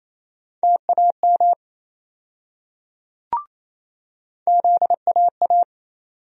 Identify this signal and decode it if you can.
Morse code, heard as a keyed tone, about 28 wpm, 700 Hz. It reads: TAM ZAA
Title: USA Call Sign Postfixes - 3 Letters 28wpm